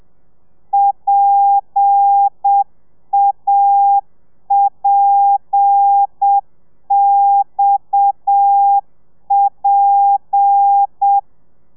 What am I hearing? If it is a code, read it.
Morse code, heard as a keyed tone, about 7 wpm, 800 Hz. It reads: PAPXP
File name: telegrafia / i2tfj / Lez7 30